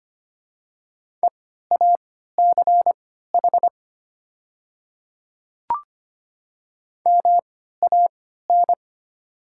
Morse code, heard as a keyed tone, about 25 wpm, spaced as for 14 wpm, 700 Hz. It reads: EACH MAN
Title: 3x Character Spacing - Top 200 Words - 25wpm